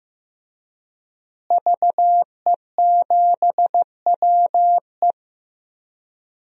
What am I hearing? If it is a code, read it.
Morse code, heard as a keyed tone, about 15 wpm, 700 Hz. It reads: VE7WE
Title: DXCC-100 Call Signs - 5 Characters - Repeated 3x - 15wpm